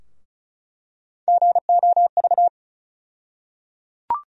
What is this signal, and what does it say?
Morse code, heard as a keyed tone, about 36 wpm, 700 Hz. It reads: GOV
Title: USA Call Sign Postfixes - 3 Letters 35wpm